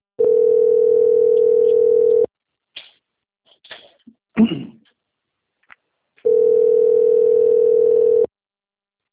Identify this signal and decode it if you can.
Ringback tone (North America), 440 + 480 Hz.